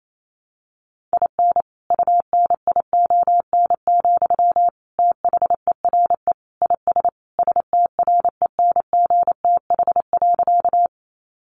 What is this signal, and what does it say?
Morse code, heard as a keyed tone, about 28 wpm, 700 Hz. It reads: IN UNION, THERE IS STRENGTH.
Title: Sentences from Top 1400 Words 28wpm